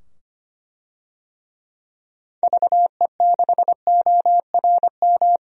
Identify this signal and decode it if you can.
Morse code, heard as a keyed tone, about 25 wpm, 700 Hz. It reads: VE6ORM
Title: DXCC-100 Call Signs - 6 Characters - 25wpm